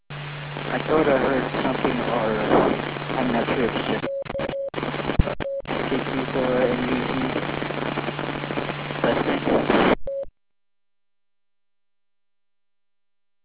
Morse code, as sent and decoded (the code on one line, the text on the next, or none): none